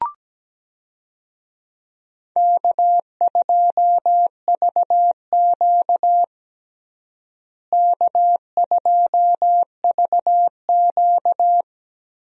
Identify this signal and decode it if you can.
Morse code, heard as a keyed tone, about 17 wpm, 700 Hz. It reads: K2VQ K2VQ
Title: USA Call Signs - 4 Characters - Repeated 2x - 17wpm